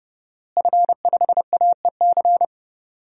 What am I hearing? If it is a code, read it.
Morse code, heard as a keyed tone, about 30 wpm, 700 Hz. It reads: F5AEC
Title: DXCC-100 Call Signs - 5 Characters - Repeated 3x - 30wpm